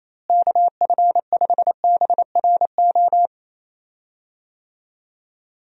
Morse code, heard as a keyed tone, about 28 wpm, 700 Hz. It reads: KF5BRO
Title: USA Call Signs - 6 Characters - Repeated 3x - 28wpm